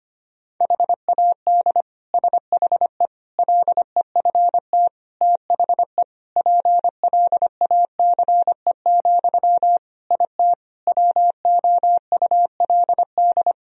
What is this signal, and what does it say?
Morse code, heard as a keyed tone, about 25 wpm, 700 Hz. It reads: HAD SHE LEFT THE PLACE, IT WOULD
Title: Sentences from Top 1000 Words - Review Part 3 - 25wpm